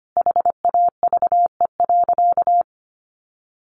Morse code, heard as a keyed tone, about 25 wpm, 700 Hz. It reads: HAVE.
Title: Sentences from Top 1100 Words 25wpm